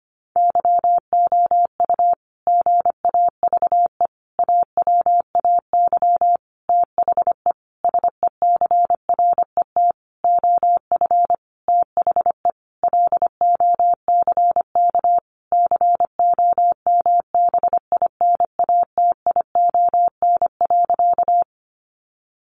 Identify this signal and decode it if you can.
Morse code, heard as a keyed tone, about 25 wpm, 700 Hz. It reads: YOU GAVE AWAY THE SECRET OF THE LOCK COMBINATION.
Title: Sentences from Top 1300 Words 25wpm